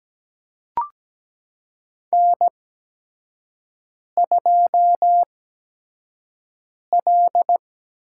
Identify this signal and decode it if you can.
Morse code, heard as a keyed tone, about 17 wpm, spaced as for 5 wpm, 700 Hz. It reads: N2L
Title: USA Call Signs - 5 Characters - 8x Character Spacing - 17wpm